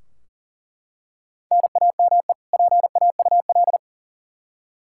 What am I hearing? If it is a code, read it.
Morse code, heard as a keyed tone, about 40 wpm, 700 Hz. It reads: NAME PAUL
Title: Practice QSO - Exchange 1 - Beginning - 40wpm